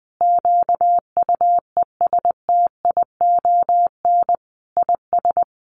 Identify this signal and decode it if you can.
Morse code, heard as a keyed tone, about 20 wpm, 700 Hz. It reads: QUESTION IS
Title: Sentences from Top 1400 Words 20wpm